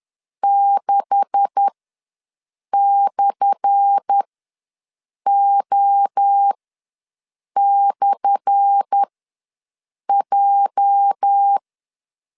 Morse code, heard as a keyed tone, about 10 wpm, spaced as for 6 wpm, 800 Hz. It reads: 6/O/J
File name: LEZIONI / 10 SIGNS / 10A SLASH ws 10wpm 4121